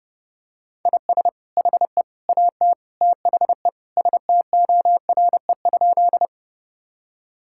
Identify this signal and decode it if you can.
Morse code, heard as a keyed tone, about 30 wpm, 700 Hz. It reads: IS HE AT THE STORE?
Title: Sentences from Top 1000 Words - Review Part 5 - 30wpm